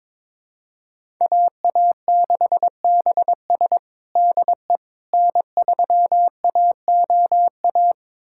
Morse code, heard as a keyed tone, about 22 wpm, 700 Hz. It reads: AA6BS DE N3AOA
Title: USA Callsign DE Callsign - 5 Characters - 22wpm